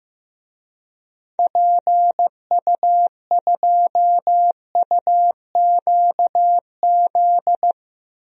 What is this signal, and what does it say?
Morse code, heard as a keyed tone, about 15 wpm, 700 Hz. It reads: PU2UQZ
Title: DXCC-100 Call Signs - 6 Characters - Repeated 3x - 15wpm